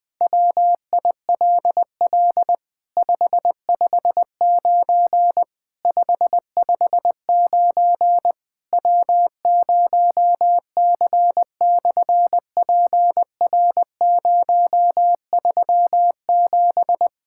Morse code, heard as a keyed tone, about 20 wpm, 700 Hz. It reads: WILL 559 559 W0C/PR037
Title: SOTA QSO Practice 20wpm